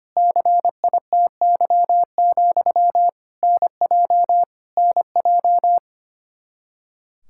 Morse code, heard as a keyed tone, about 25 wpm, 700 Hz. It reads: CITY, NJ NJ
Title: Practice QSO - Exchange 1 - Beginning - 25wpm